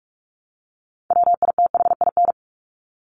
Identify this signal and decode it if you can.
Morse code, heard as a keyed tone, about 44 wpm, 700 Hz. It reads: WITHIN
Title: Top 501-750 Words 45wpm